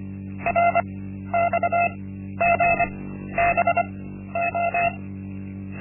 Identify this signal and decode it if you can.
Morse code, heard as a keyed tone, about 25 wpm, spaced as for 12 wpm, 690 Hz. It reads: RXGBO
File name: Linksmakalnis-1